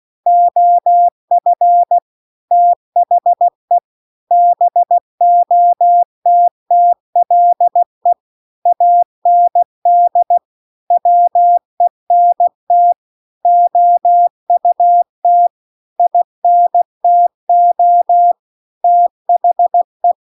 Morse code, heard as a keyed tone, about 16 wpm, 700 Hz. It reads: OF THE BOTTLE AND WENT OUT INTO THE
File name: CW-The-Big-Twoheaded-River-2-16WPM